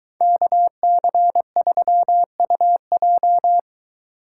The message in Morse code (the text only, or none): KC3UJ